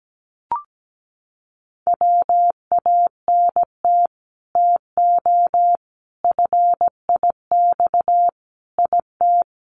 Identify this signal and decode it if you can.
Morse code, heard as a keyed tone, about 17 wpm, 700 Hz. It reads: WANT TO FIX IT
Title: Sentences from Top 1400 Words 17wpm